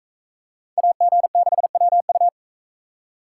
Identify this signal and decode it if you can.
Morse code, heard as a keyed tone, about 41 wpm, 700 Hz. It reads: AG6WU